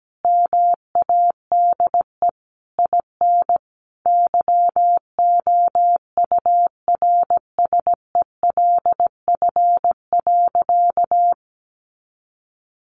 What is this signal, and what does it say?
Morse code, heard as a keyed tone, about 17 wpm, 700 Hz. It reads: MADE IN YOURSELF.
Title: Sentences from Top 600 Words 17wpm